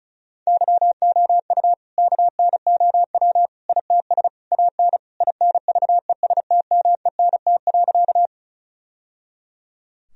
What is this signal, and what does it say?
Morse code, heard as a keyed tone, about 34 wpm, 700 Hz. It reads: YOU KNOW ITS AN INVESTMENT.
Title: Sentences from Top 1000 Words - Review Part 4 - 35wpm